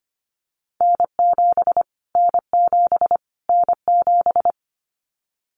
Morse code, heard as a keyed tone, about 25 wpm, 700 Hz. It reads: N7 N7 N7
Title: Callsign Prefixes for Top 25 DXCC - Repeated 3x - 25wpm